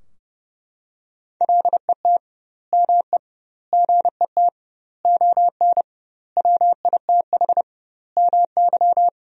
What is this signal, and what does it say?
Morse code, heard as a keyed tone, about 30 wpm, 700 Hz. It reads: LET ME GET ON WITH MY
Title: 2x Word-Spacing - Sentences from Top 300 Words - 30wpm